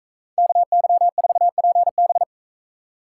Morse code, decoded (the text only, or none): KY4PB